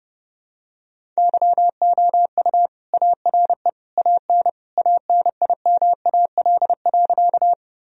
Morse code, heard as a keyed tone, about 30 wpm, 700 Hz. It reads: YOU ARE AN ANIMAL.
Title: Sentences from Top 1000 Words - Review Part 1 - 30wpm